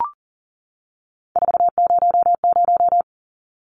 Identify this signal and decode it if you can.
Morse code, heard as a keyed tone, about 40 wpm, 700 Hz. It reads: 400
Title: Numbers - 3-Digit 40wpm